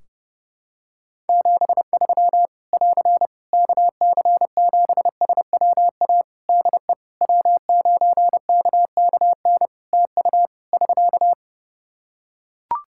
Morse code, heard as a keyed tone, about 32 wpm, 700 Hz. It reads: 73 <AR> KC7SWA DE W9KKN TU <SK>
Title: Practice QSO - Exchange 4 - Ending - 30wpm